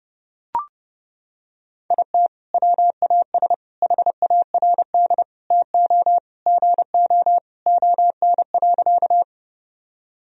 Morse code, heard as a keyed tone, about 30 wpm, 700 Hz. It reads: IT WAS HARD TO GO ON.
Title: Sentences from Top 1000 Words - Review Part 1 - 30wpm